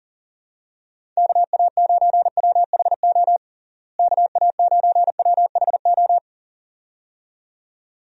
Morse code, heard as a keyed tone, about 40 wpm, 700 Hz. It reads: KA9WHO KA9WHO
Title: USA Call Signs - 6 Characters - Repeated 2x - 40wpm